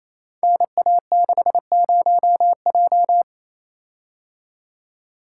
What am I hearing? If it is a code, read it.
Morse code, heard as a keyed tone, about 29 wpm, 700 Hz. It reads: NA60J
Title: Code Groups - 5 Letter:Numbers - 28wpm